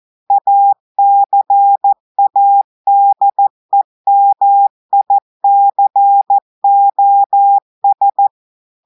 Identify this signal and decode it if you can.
Morse code, heard as a keyed tone, about 14 wpm, 800 Hz. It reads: ACADEMICOS
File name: audios / portugais / portugais texte en clair 1 vitesse 14 mots minutes